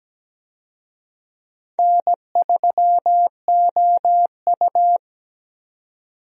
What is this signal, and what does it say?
Morse code, heard as a keyed tone, about 17 wpm, 700 Hz. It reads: N3OU
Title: USA Call Signs - 4 Characters - 17wpm